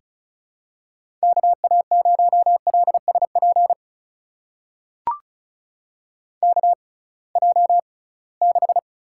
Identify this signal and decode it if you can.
Morse code, heard as a keyed tone, about 36 wpm, spaced as for 12 wpm, 700 Hz. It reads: KA0LS PKJ6
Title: USA Call Signs - 6x Character Spacing - 35wpm